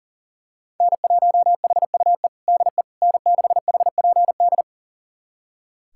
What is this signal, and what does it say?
Morse code, heard as a keyed tone, about 37 wpm, 700 Hz. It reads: N1HUE DE N6HPD
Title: USA Callsign DE Callsign 40wpm